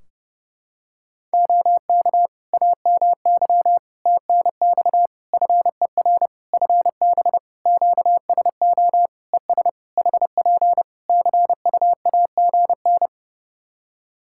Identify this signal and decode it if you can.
Morse code, heard as a keyed tone, about 30 wpm, 700 Hz. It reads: OK AMY TNX FER FB QSO ES HP CUAGN
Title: Practice QSO - Exchange 4 - Ending - 30wpm